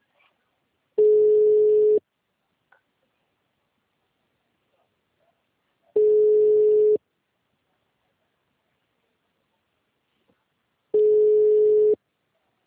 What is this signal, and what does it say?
Ringback tone (Europe, 425 Hz).